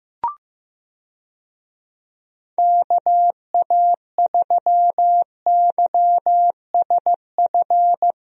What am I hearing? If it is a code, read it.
Morse code, heard as a keyed tone, about 15 wpm, 700 Hz. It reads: KA3YSF
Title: USA Call Signs - 6 Characters - Repeated 3x - 15wpm